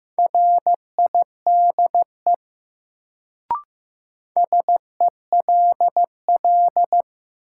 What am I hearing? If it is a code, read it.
Morse code, heard as a keyed tone, about 15 wpm, 700 Hz. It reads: RIDE SELL
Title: Sentences from Top 1000 Words - Review Part 6 - New Word Drill - 15wpm